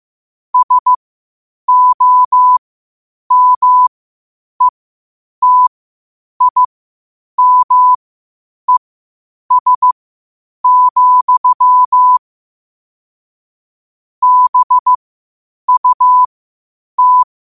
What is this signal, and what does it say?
Morse code, heard as a keyed tone, about 15 wpm, spaced as for 8 wpm, 1000 Hz. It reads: SOMETIMES, BUT
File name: qotdfiles / 8wpm